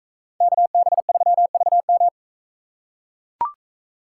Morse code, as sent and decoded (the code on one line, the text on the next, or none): -.- -... ...-- ...- --
KB3VM